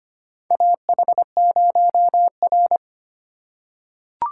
Morse code, heard as a keyed tone, about 25 wpm, 700 Hz. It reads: AH0R
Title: USA Call Signs - 4 Characters - 2x Character Spacing - 25wpm